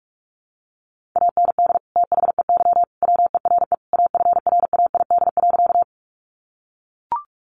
Morse code, heard as a keyed tone, about 44 wpm, 700 Hz. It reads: AND THEY WERE AFRAID.
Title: Sentences from Top 1500 Words 45wpm